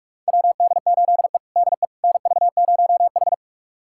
Morse code, heard as a keyed tone, about 46 wpm, 700 Hz. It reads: WD8E DE NV0H
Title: USA Callsign DE Callsign - 4 Characters - 45wpm